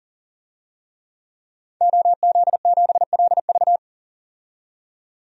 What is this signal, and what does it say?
Morse code, heard as a keyed tone, about 40 wpm, 700 Hz. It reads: OZ7LV
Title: DXCC-100 Call Signs - 5 Characters - 40wpm